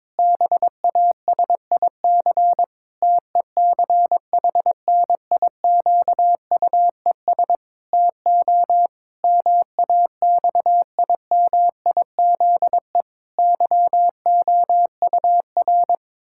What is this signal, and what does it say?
Morse code, heard as a keyed tone, about 22 wpm, 700 Hz. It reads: BASIC TECHNIQUES TO MAXIMIZE YOUR
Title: Pangram Challenge v2 - 22wpm